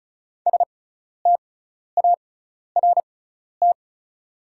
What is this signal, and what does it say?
Morse code, heard as a keyed tone, about 34 wpm, spaced as for 12 wpm, 690 Hz. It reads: START
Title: 6x Character Spacing - Top 500 Words - 35wpm